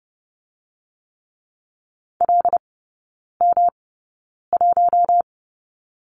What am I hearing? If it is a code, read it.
Morse code, heard as a keyed tone, about 30 wpm, spaced as for 9 wpm, 700 Hz. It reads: LM1
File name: Lesson 18e - Letter m - Three Characters 30wpm